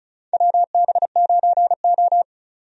Morse code, heard as a keyed tone, about 34 wpm, 700 Hz. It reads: WB9O
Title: USA Call Signs - 4 Characters - 35wpm